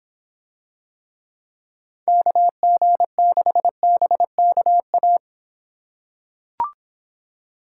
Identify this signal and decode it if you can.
Morse code, heard as a keyed tone, about 26 wpm, 700 Hz. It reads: KG6BKA